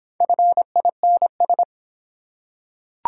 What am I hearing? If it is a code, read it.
Morse code, heard as a keyed tone, about 26 wpm, 700 Hz. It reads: FINS